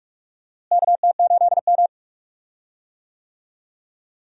Morse code, heard as a keyed tone, about 46 wpm, 700 Hz. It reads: KT8M
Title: USA Call Signs - 4 Characters - Repeated 3x - 45wpm